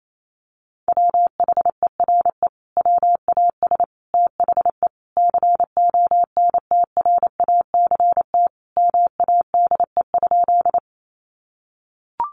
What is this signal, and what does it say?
Morse code, heard as a keyed tone, about 28 wpm, 700 Hz. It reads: WHERE WAS THE CONTRACT MADE?
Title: Sentences from Top 1400 Words 28wpm